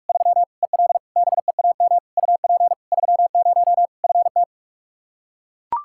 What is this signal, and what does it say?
Morse code, heard as a keyed tone, about 41 wpm, 700 Hz. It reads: 3 EL BEAM UP 30 FT
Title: Practice QSO - Exchange 2 - Info - 45wpm